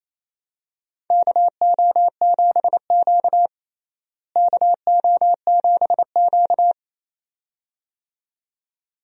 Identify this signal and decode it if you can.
Morse code, heard as a keyed tone, about 28 wpm, 700 Hz. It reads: KO7Q KO7Q
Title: USA Call Signs - 4 Characters - Repeated 2x - 28wpm